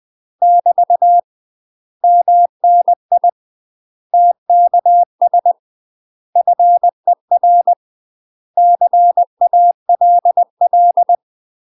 Morse code, heard as a keyed tone, about 20 wpm, 700 Hz. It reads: <BT> MNI TKS FER CALL
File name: UK MASTER QSOs TRAINING / 20WPM / UK MASTER QSOs 0070